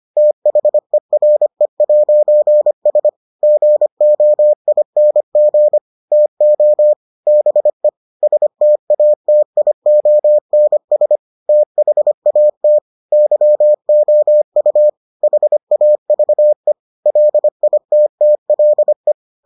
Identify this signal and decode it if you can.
Morse code, heard as a keyed tone, about 25 wpm, 600 Hz. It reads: THERE'S GOING TO BE STATIONS THAT YOU HAVE LITTLE